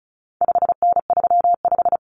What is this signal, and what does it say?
Morse code, heard as a keyed tone, about 34 wpm, 700 Hz. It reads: 5N35
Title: Common Cut Numbers - 4-Digit 35wpm